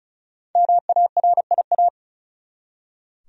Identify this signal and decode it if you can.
Morse code, heard as a keyed tone, about 36 wpm, 700 Hz. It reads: MARIA
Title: USA Common Names - 35wpm